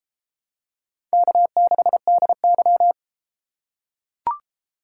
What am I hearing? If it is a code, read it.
Morse code, heard as a keyed tone, about 31 wpm, 700 Hz. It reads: K6DY